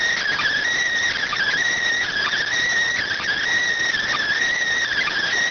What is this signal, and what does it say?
Emergency Alert System (SAME) tones, heard as harsh buzzing bursts. It is heard from 1.2 to 5.3 s.